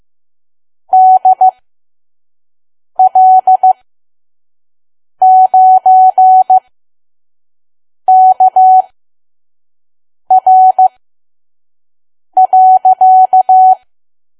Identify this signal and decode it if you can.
Morse code, heard as a keyed tone, about 15 wpm, spaced as for 5 wpm, 750 Hz. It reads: DL9KR.
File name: Archive / 5 WPM files / PM